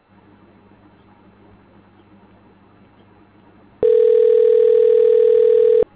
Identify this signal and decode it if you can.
Ringback tone (North America), 440 + 480 Hz.